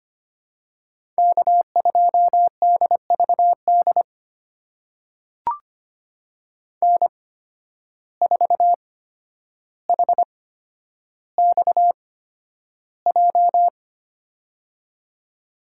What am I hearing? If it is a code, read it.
Morse code, heard as a keyed tone, about 25 wpm, spaced as for 11 wpm, 700 Hz. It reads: K2DVD N4HXJ
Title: USA Call Signs - 5 Characters - 8x Character Spacing - 25wpm